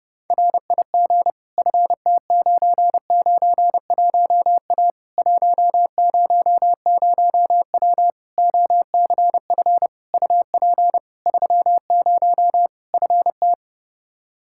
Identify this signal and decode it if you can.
Morse code, heard as a keyed tone, about 30 wpm, 700 Hz. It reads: RIG FT991A 100W OCF UP 30 FT
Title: Practice QSO - Exchange 2 - Info - 30wpm